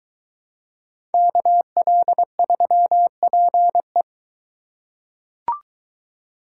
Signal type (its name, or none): Morse code